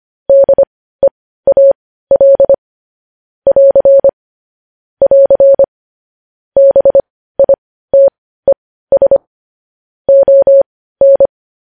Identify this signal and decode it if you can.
Morse code, heard as a keyed tone, about 25 wpm, spaced as for 15 wpm, 550 Hz. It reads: DEAL <AR> <AR> BITES ON
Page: news headlines in morse code at 15 wpm